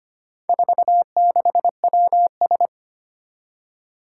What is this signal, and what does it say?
Morse code, heard as a keyed tone, about 25 wpm, 700 Hz. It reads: 46WS
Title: Four Challenging Letter-Number Combinations 25wpm